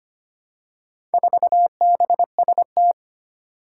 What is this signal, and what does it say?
Morse code, heard as a keyed tone, about 25 wpm, 700 Hz. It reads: 4BST
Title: Four Challenging Letter-Number Combinations 25wpm